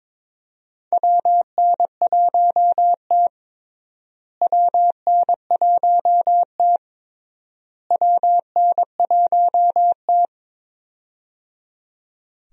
Morse code, heard as a keyed tone, about 22 wpm, 700 Hz. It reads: WN1T WN1T WN1T
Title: USA Call Signs - 4 Characters - Repeated 3x - 22wpm